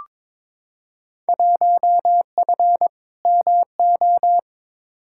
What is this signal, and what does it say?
Morse code, heard as a keyed tone, about 22 wpm, 700 Hz. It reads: 1F MO
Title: Field Day Exchanges Encore - 22wpm